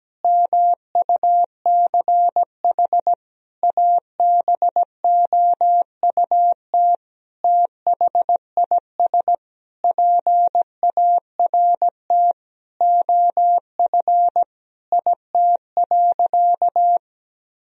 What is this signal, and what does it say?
Morse code, heard as a keyed tone, about 17 wpm, 700 Hz. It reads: MUCH ABOUT THIS PART OF IT.
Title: Sentences from Top 1000 Words - Review Part 1 - 17wpm